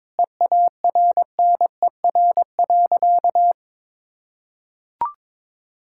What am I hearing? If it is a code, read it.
Morse code, heard as a keyed tone, about 21 wpm, 700 Hz. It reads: EARNER.
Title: Pangram Challenge v2 - 22wpm